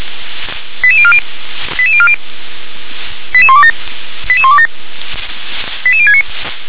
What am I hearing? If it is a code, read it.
Five-tone selective call (ZVEI), heard as quick runs of five tones, tones of about 70 ms. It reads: 88040 88040 88127 88127 88069
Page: SDR/ZVEI